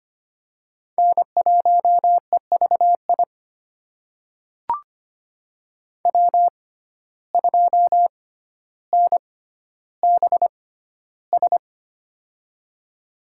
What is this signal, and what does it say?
Morse code, heard as a keyed tone, about 25 wpm, spaced as for 13 wpm, 700 Hz. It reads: N1EVI W2NBS
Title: USA Call Signs - 5 Characters - 6x Character Spacing - 25wpm